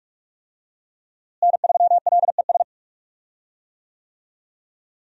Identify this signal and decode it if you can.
Morse code, heard as a keyed tone, about 47 wpm, 700 Hz. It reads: N3LES
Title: USA Call Signs - 5 Characters - Repeated 2x - 45wpm